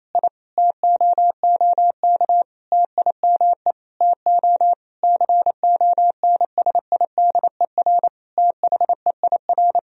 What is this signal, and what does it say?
Morse code, heard as a keyed tone, about 28 wpm, 700 Hz. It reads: I TOOK TIME TO CONSIDER THEIR
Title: Sentences from Top 1000 Words - Review Part 4 - 28wpm